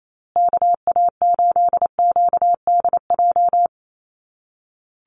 Morse code, heard as a keyed tone, about 28 wpm, 700 Hz. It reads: KA8QDJ
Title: USA Call Signs Repeated 3x 28wpm